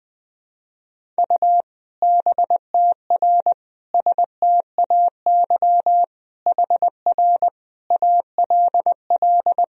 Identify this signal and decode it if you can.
Morse code, heard as a keyed tone, about 20 wpm, 700 Hz. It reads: U BTR STAY HR ALL
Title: Sentences from Top 500 Words and Abbreviations 20wpm